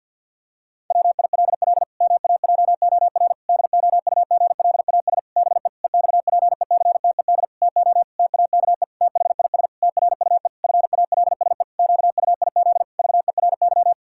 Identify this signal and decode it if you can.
Morse code, heard as a keyed tone, about 50 wpm, 700 Hz. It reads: WILL MAJOR DOUGLAS BE EXPECTED TO TAKE THIS TRUE FALSE QUIZ VERY
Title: Pangram Challenge v2 - 50wpm